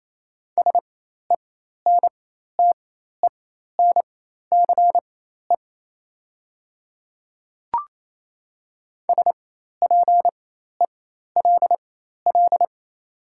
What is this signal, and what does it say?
Morse code, heard as a keyed tone, about 28 wpm, spaced as for 13 wpm, 700 Hz. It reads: SENTENCE SPELL
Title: 4x Character Spacing - Top 500 Words - 28wpm